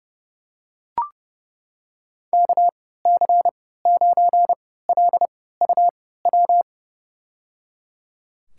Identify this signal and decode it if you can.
Morse code, heard as a keyed tone, about 30 wpm, spaced as for 17 wpm, 700 Hz. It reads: KC9LUW
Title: USA Call Signs - 6 Characters - 3x Character Spacing - 30wpm